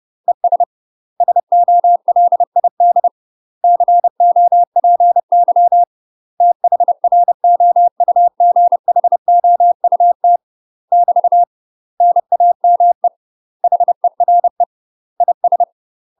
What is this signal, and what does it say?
Morse code, heard as a keyed tone, about 30 wpm, 700 Hz. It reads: ES SOLID COPY THROUGHOUT <BT> NAME HERE IS
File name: UK MASTER QSOs TRAINING / 30WPM / UK MASTER QSOs 0024